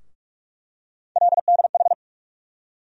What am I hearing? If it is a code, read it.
Morse code, heard as a keyed tone, about 46 wpm, 700 Hz. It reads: LDH
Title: USA Call Sign Postfixes - 3 Letters 45wpm